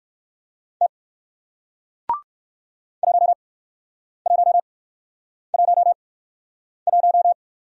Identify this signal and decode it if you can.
Morse code, heard as a keyed tone, about 60 wpm, spaced as for 9 wpm, 700 Hz. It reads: T 1111